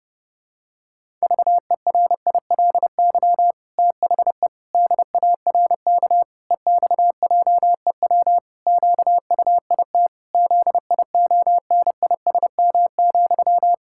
Morse code, heard as a keyed tone, about 30 wpm, 700 Hz. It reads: VERILY THE DARK EXJEW QUIT ZIONISM,
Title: Pangram Challenge v2 - 30wpm